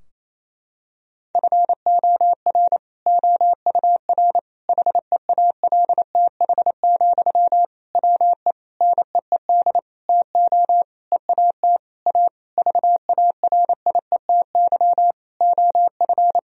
Morse code, heard as a keyed tone, about 27 wpm, 700 Hz. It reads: FOR OUR HEALTH, WE NEED TO EAT A VARIETY OF
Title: Sentences from Top 1000 Words - Review Part 4 - 28wpm